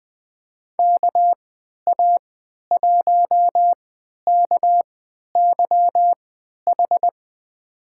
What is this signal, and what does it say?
Morse code, heard as a keyed tone, about 20 wpm, spaced as for 11 wpm, 700 Hz. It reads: KA1KYH
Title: USA Call Signs - 3x Character Spacing - 20wpm